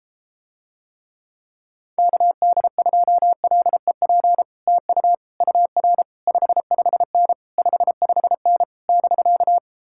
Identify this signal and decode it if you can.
Morse code, heard as a keyed tone, about 33 wpm, 700 Hz. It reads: KD2LEP TU UR 55N 55N <BK>